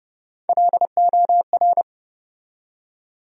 Morse code, heard as a keyed tone, about 30 wpm, 700 Hz. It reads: LOR
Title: USA Call Sign Postfixes - 3 Letters 30wpm